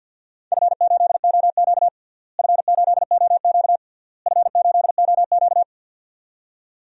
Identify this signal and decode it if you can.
Morse code, heard as a keyed tone, about 53 wpm, 700 Hz. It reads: F8OQ F8OQ F8OQ
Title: DXCC-100 Call Signs - 4 Characters - Repeated 3x - 50wpm